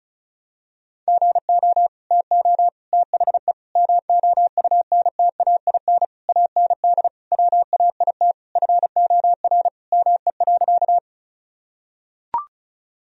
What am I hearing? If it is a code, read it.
Morse code, heard as a keyed tone, about 36 wpm, 700 Hz. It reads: GO TO THE MOUNTAIN AND WAIT FOR ME.
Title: Sentences from Top 1000 Words - Review Part 1 - 35wpm